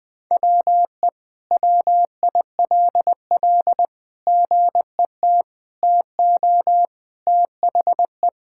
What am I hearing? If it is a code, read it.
Morse code, heard as a keyed tone, about 21 wpm, 700 Hz. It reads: WE WILL GET TO THE
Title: Sentences with 6-Word Phrases - Spoken Beginning - 20wpm